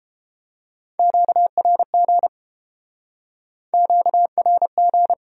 Morse code, heard as a keyed tone, about 32 wpm, 700 Hz. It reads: QRG QRG